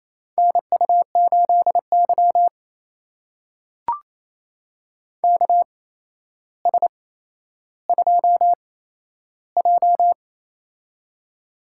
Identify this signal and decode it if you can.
Morse code, heard as a keyed tone, about 28 wpm, spaced as for 12 wpm, 700 Hz. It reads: NU8Y KS2J
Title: USA Call Signs - 4 Characters - 8x Character Spacing - 28wpm